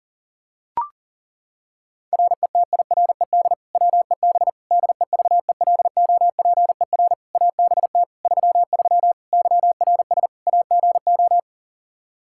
Morse code, heard as a keyed tone, about 40 wpm, 700 Hz. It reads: RETIRED WEB DEVELOPER ABT 33 YRS AGO
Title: K1USN SST QSO Practice 40wpm